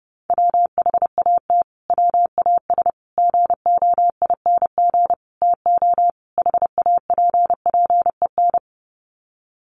Morse code, heard as a keyed tone, about 30 wpm, 700 Hz. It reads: WHAT WAS GOING TO HAPPEN
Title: Sets of 5 Words Encore 30wpm